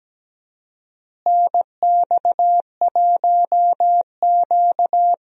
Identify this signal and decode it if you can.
Morse code, heard as a keyed tone, about 17 wpm, 700 Hz. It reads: NX1Q
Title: USA Call Signs - 4 Characters - Repeated 3x - 17wpm